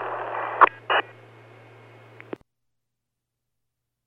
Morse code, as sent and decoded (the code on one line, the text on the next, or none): none